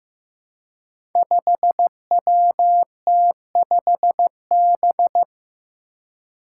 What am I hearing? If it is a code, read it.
Morse code, heard as a keyed tone, about 15 wpm, 700 Hz. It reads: 5WT5B
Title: Five Challenging Letter-Number Combinations 15wpm